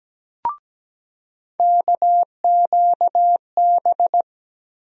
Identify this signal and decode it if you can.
Morse code, heard as a keyed tone, about 17 wpm, 700 Hz. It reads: KQB
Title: USA Call Sign Postfixes - 3 Letters 17wpm